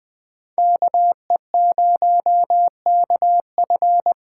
Morse code, heard as a keyed tone, about 20 wpm, 700 Hz. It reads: KE0KF
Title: USA Call Signs - 6 Characters - 6x Character Spacing - 20wpm